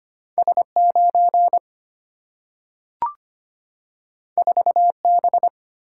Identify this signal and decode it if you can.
Morse code, heard as a keyed tone, about 25 wpm, 700 Hz. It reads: S9 4B
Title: Single-Letter Single-Number 25wpm